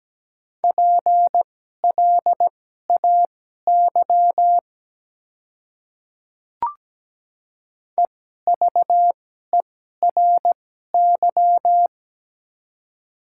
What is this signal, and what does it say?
Morse code, heard as a keyed tone, about 17 wpm, spaced as for 12 wpm, 700 Hz. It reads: PLAY EVERY
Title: 2x Character Spacing - Top 200 Words - 17wpm